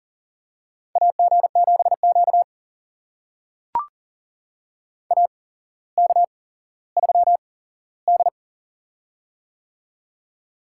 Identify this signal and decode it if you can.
Morse code, heard as a keyed tone, about 40 wpm, spaced as for 17 wpm, 700 Hz. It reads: AG7Q AK3D
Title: USA Call Signs - 4 Characters - 8x Character Spacing - 40wpm